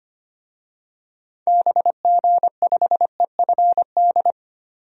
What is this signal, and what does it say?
Morse code, heard as a keyed tone, about 25 wpm, 700 Hz. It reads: BG5EFD
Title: DXCC-100 Call Signs - 6 Characters - 25wpm